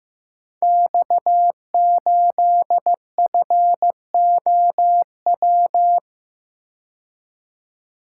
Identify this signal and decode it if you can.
Morse code, heard as a keyed tone, about 15 wpm, 700 Hz. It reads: X8FOW
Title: Code Groups - 5 Letter:Numbers - 15wpm